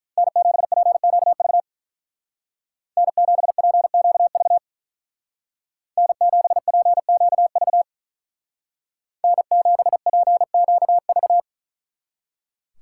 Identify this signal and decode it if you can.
Morse code, heard as a keyed tone, about 42 wpm, 700 Hz. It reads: N7PQV N7PQV N7PQV N7PQV
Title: Speed Racing - USA Call Signs 35wpm